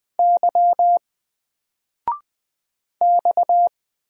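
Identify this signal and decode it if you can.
Morse code, heard as a keyed tone, about 20 wpm, 700 Hz. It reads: Y X
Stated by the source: Single letter and number Morse drill at 20 wpm, from N4LCD.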